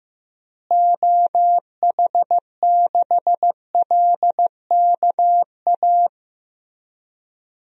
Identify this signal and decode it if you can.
Morse code, heard as a keyed tone, about 15 wpm, 700 Hz. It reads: OH6LKA
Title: DXCC-100 Call Signs - 6 Characters - Repeated 3x - 15wpm